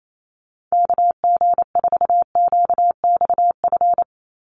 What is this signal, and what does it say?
Morse code, heard as a keyed tone, about 28 wpm, 700 Hz. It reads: KG4QXF
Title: USA Call Signs 28wpm